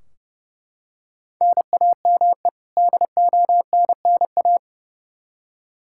Morse code, heard as a keyed tone, about 30 wpm, 700 Hz. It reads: NAME DONNA
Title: Practice QSO - Exchange 1 - Beginning - 30wpm